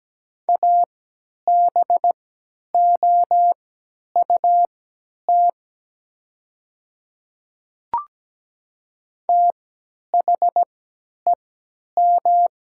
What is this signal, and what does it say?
Morse code, heard as a keyed tone, about 17 wpm, spaced as for 10 wpm, 700 Hz. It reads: ABOUT THEM
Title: 3x Character Spacing - Top 100 Words - 17wpm